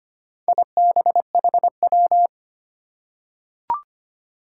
Morse code, heard as a keyed tone, about 25 wpm, 700 Hz. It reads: IBHW
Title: Four Challenging Letter-Number Combinations 25wpm